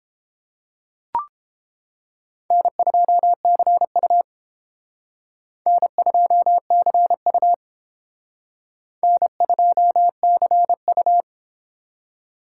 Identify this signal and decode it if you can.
Morse code, heard as a keyed tone, about 30 wpm, 700 Hz. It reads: N2CU N2CU N2CU